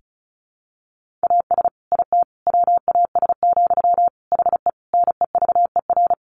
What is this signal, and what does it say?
Morse code, heard as a keyed tone, about 34 wpm, 700 Hz. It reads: AS IT WAS, HE NEVER
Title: Sentences from Top 1400 Words 35wpm